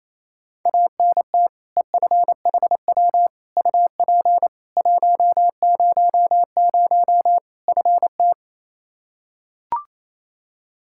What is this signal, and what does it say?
Morse code, heard as a keyed tone, about 28 wpm, 700 Hz. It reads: ANT EFHW UP 100 FT
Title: Practice QSO - Exchange 2 - Info - 28wpm